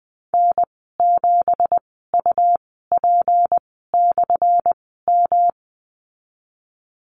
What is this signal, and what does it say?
Morse code, heard as a keyed tone, about 20 wpm, spaced as for 15 wpm, 700 Hz. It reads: N7UP/M
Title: 179 Lesson 29g - Slash - Call Signs 20wpm